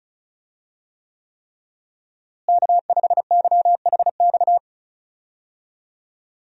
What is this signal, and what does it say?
Morse code, heard as a keyed tone, about 36 wpm, 700 Hz. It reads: K5YHX
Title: USA Call Signs - 5 Characters - Repeated 3x - 35wpm